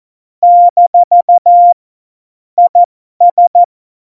Morse code, heard as a keyed tone, about 14 wpm, spaced as for 12 wpm, 700 Hz. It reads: - IS
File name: Morsekurs / Witze